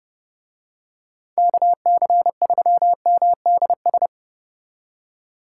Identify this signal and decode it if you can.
Morse code, heard as a keyed tone, about 30 wpm, 700 Hz. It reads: KC3MDS